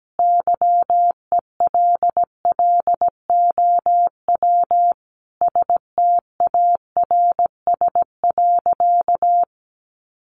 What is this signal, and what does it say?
Morse code, heard as a keyed tone, about 17 wpm, 700 Hz. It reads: YELLOW STARS.
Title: Additional Easy Phrases 17wpm